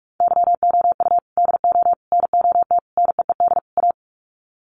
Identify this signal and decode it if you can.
Morse code, heard as a keyed tone, about 44 wpm, 700 Hz. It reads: YOU DO NOT NEED A
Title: Sets of 5 Words Encore 45wpm